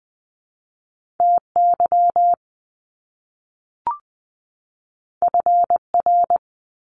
Morse code, heard as a keyed tone, about 20 wpm, 700 Hz. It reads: TY FR
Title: Top Two Letter Combinations in Words 20wpm